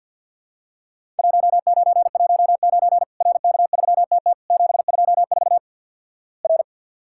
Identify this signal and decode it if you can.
Morse code, heard as a keyed tone, about 50 wpm, 700 Hz. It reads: 1919 RK3TT 724 R
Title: Worked All Europe DX Contest - QTC Exchange - 50wpm